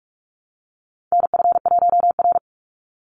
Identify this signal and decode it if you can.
Morse code, heard as a keyed tone, about 44 wpm, 700 Hz. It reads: NF1R
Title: NA SOTA Chaser Callsigns 45wpm